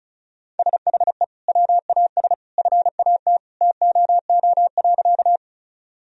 Morse code, heard as a keyed tone, about 36 wpm, 700 Hz. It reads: SHE WAS FAT TOO.
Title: Sentences from Top 1000 Words - Review Part 4 - 35wpm